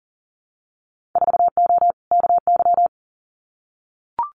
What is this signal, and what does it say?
Morse code, heard as a keyed tone, about 41 wpm, 700 Hz. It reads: 4O KY
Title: Winter Field Day Exchanges Encore 40wpm